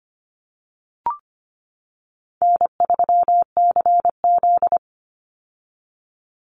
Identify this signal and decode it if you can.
Morse code, heard as a keyed tone, about 25 wpm, 700 Hz. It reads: N3CZ
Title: Easy Call Signs - 25wpm